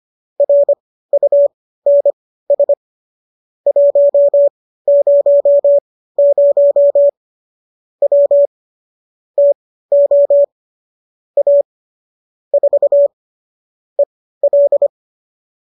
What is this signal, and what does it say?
Morse code, heard as a keyed tone, about 25 wpm, spaced as for 15 wpm, 570 Hz. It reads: RUNS 100 W TO A 4 EL